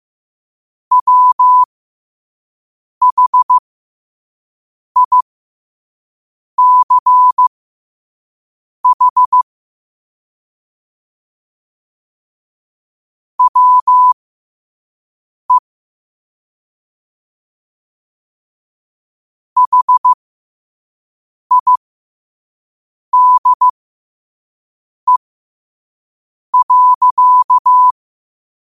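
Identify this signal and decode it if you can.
Morse code, heard as a keyed tone, about 15 wpm, spaced as for 5 wpm, 1000 Hz. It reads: WHICH WE HIDE.
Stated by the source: Morse code sent at 5 wpm.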